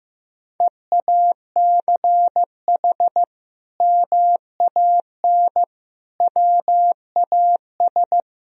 Morse code, heard as a keyed tone, about 15 wpm, 700 Hz. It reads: EACH MAN WAS
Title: Sentences from Top 1000 Words - Review Part 2 - 15wpm